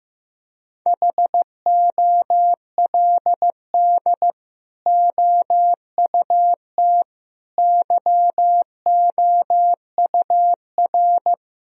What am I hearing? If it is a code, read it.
Morse code, heard as a keyed tone, about 15 wpm, 700 Hz. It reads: HOLD OUT YOUR
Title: Sentences from Top 1000 Words - Review Part 2 - 15wpm